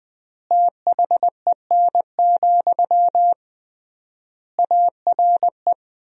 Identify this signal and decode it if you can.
Morse code, heard as a keyed tone, about 20 wpm, 700 Hz. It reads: THEN, ARE
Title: 3x Word-Spacing - Sentences from Top 100 Words - 20wpm